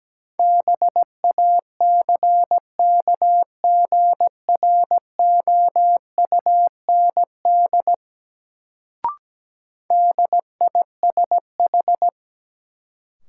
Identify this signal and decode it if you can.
Morse code, heard as a keyed tone, about 17 wpm, 700 Hz. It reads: BACKGROUND DISH
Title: Sentences from Top 1300 Words - New Word Drill Alt - 17wpm